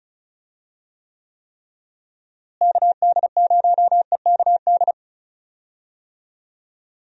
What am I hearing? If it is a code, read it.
Morse code, heard as a keyed tone, about 34 wpm, 700 Hz. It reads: KD0EKD
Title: USA Call Signs - 6 Characters - Repeated 3x - 35wpm